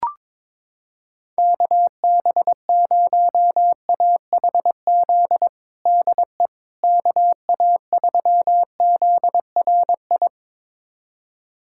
Morse code, heard as a keyed tone, about 22 wpm, 700 Hz. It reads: KB0AHZ DE KA3ZRI
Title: USA Callsign DE Callsign - 6 Characters - 22wpm